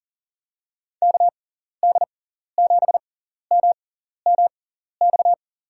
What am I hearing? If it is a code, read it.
Morse code, heard as a keyed tone, about 40 wpm, spaced as for 14 wpm, 700 Hz. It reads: KD7MMX
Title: USA Call Signs - 6 Characters - 6x Character Spacing - 40wpm